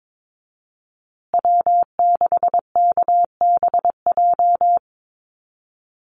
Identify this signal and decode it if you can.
Morse code, heard as a keyed tone, about 22 wpm, 700 Hz. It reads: W6KBJ
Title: USA Call Signs Repeated 3x 22wpm